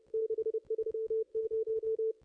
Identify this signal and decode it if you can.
Morse code, heard as a keyed tone, about 29 wpm, 440 Hz. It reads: 630
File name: contests / ss-cw-k5tr-op-n5rz -0600